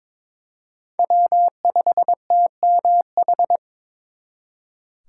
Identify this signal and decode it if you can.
Morse code, heard as a keyed tone, about 26 wpm, 700 Hz. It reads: W5TMH
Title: Speed Racing - USA Call Signs 22wpm